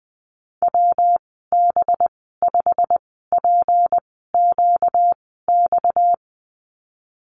Morse code, heard as a keyed tone, about 20 wpm, spaced as for 14 wpm, 700 Hz. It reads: WB5PQX
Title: 238 Lesson 38g - Letter q - Call Signs 20wpm